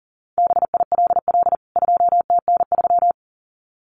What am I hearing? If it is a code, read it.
Morse code, heard as a keyed tone, about 40 wpm, 700 Hz. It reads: BILL 2TN3
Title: CWT Exchange 40wpm